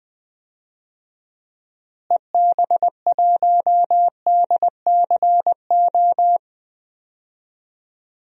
Morse code, heard as a keyed tone, about 20 wpm, 700 Hz. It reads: EB1DCO
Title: DXCC-100 Call Signs - 6 Characters - Repeated 3x - 20wpm